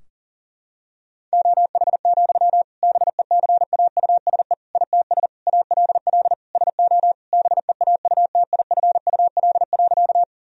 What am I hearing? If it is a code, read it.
Morse code, heard as a keyed tone, about 37 wpm, 700 Hz. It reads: OH, BECAUSE ITS ALL SO BEAUTIFUL.
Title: Sentences from Top 1000 Words - Review Part 3 - 40wpm